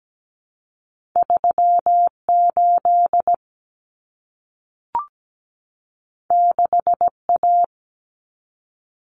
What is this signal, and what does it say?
Morse code, heard as a keyed tone, about 17 wpm, 700 Hz. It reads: 38 6A
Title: Common Cut Numbers - 2-Digit 17wpm